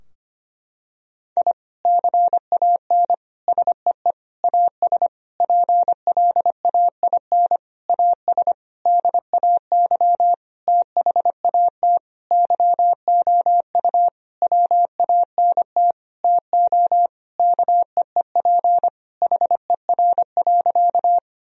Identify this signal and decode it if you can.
Morse code, heard as a keyed tone, about 25 wpm, 700 Hz. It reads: I CAN SEE AS PLAIN AS DAY THAT YOU WANT TO KEEP HER.
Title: Sentences from Top 1000 Words - Review Part 2 - 25wpm